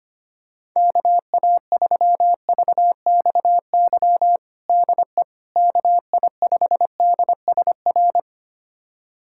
Morse code, heard as a keyed tone, about 25 wpm, 700 Hz. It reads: KA3VXY DE KI5DSR
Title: USA Callsign DE Callsign 25wpm